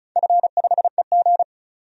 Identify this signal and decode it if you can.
Morse code, heard as a keyed tone, about 36 wpm, 700 Hz. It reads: F5EG
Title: DXCC-100 Call Signs - 4 Characters - 35wpm